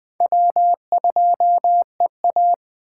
Morse code, heard as a keyed tone, about 24 wpm, 700 Hz. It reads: W2EA
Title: Speed Racing - USA 1x2 Call Signs 20wpm